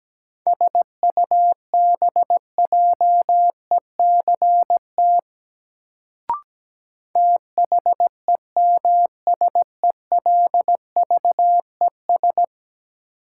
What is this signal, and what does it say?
Morse code, heard as a keyed tone, about 17 wpm, 700 Hz. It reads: SUBJECT THEMSELVES
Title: Sentences from Top 800 Words - New Word Drill - 17wpm